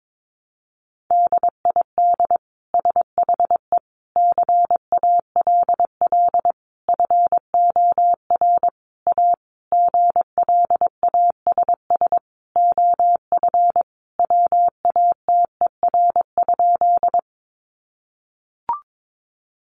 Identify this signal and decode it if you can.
Morse code, heard as a keyed tone, about 22 wpm, 700 Hz. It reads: DID SHE CALL FOR A GLASS OF WATER?
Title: Sentences from Top 1300 Words 22wpm